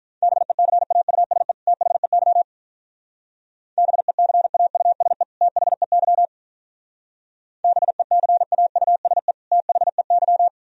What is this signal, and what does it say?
Morse code, heard as a keyed tone, about 44 wpm, 700 Hz. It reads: BECAUSE THEY BECAUSE THEY BECAUSE THEY
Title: Speed-Racing - Sets of 2 Words 35wpm